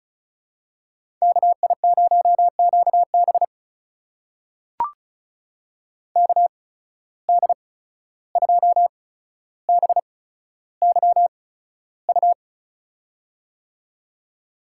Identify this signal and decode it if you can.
Morse code, heard as a keyed tone, about 34 wpm, spaced as for 14 wpm, 700 Hz. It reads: KI0QB KD2BYU
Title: USA Call Signs - 8x Character Spacing - 35wpm